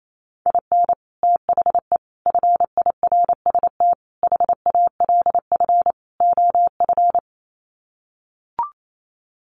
Morse code, heard as a keyed tone, about 28 wpm, 700 Hz. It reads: IN THE FIRST HALF OF
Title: Sets of 5 Words 28wpm